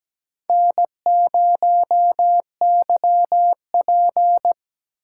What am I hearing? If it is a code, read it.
Morse code, heard as a keyed tone, about 17 wpm, 700 Hz. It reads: N0YP
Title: USA Call Signs - 4x Character Spacing - 17wpm